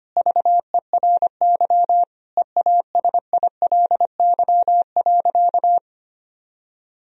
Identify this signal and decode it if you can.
Morse code, heard as a keyed tone, about 25 wpm, 700 Hz. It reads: VERY EASILY.
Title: Sentences from Top 1000 Words - Review Part 1 - 25wpm